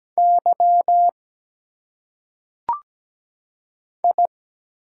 Morse code, heard as a keyed tone, about 17 wpm, 700 Hz. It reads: Y I
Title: Single Letters - Phonetic Pronunciation 17wpm